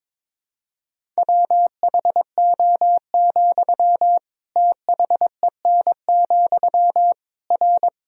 Morse code, heard as a keyed tone, about 22 wpm, 700 Hz. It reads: WHO, THEN, R
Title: Sentences from Top 100 Words and Abbreviations 22wpm